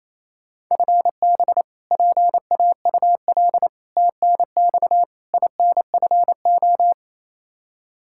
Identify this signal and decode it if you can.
Morse code, heard as a keyed tone, about 28 wpm, 700 Hz. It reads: FB PAUL TNX INFO
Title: Practice QSO - Exchange 3 - More Info - 28wpm